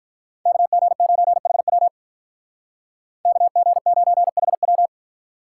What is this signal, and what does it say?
Morse code, heard as a keyed tone, about 41 wpm, 700 Hz. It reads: KG9HW KG9HW
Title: Speed Racing - USA 2x2 Call Signs 35wpm